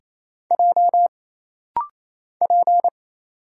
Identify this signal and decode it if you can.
Morse code, heard as a keyed tone, about 27 wpm, 700 Hz. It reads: J P
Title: Single Letter-Number - Rapid-Fire 28wpm